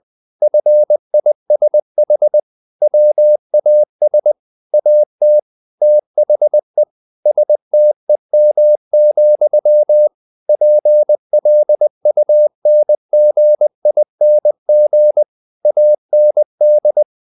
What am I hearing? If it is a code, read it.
Morse code, heard as a keyed tone, about 20 wpm, 600 Hz. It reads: FISH WAS AT THE STEM, PLUNGING AND
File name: the old man and the sea part 2 20wpm